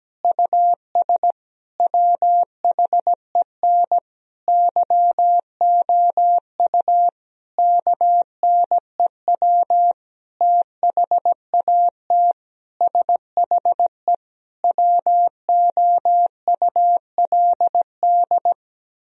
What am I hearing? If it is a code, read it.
Morse code, heard as a keyed tone, about 17 wpm, 700 Hz. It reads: US WHEN YOU KNEW THAT SHE WOULD
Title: Sentences from Top 1000 Words - Review Part 2 - 17wpm